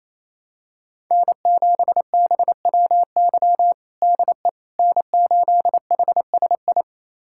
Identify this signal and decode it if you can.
Morse code, heard as a keyed tone, about 27 wpm, 700 Hz. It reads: N7BWY DE N8HSI
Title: USA Callsign DE Callsign - 5 Characters - 28wpm